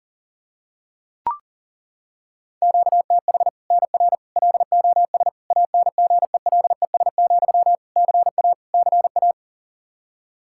Morse code, heard as a keyed tone, about 40 wpm, 700 Hz. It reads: QTH NR LOS ANGELES, CA CA
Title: Practice QSO - Exchange 1 - Beginning - 40wpm